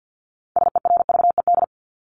Morse code, heard as a keyed tone, about 50 wpm, 700 Hz. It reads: SERVED
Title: ED Suffix - Top 100 Words with Base Words 50wpm